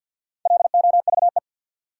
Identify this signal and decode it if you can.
Morse code, heard as a keyed tone, about 47 wpm, 700 Hz. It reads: LOVE
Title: Sets of 5 Words Encore - New Word Drill - 50wpm